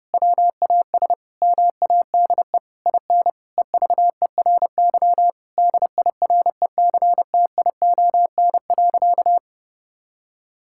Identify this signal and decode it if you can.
Morse code, heard as a keyed tone, about 28 wpm, 700 Hz. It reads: WAS MADE IN EVERY DIRECTION.
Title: Sentences from Top 1000 Words - Review Part 5 - 30wpm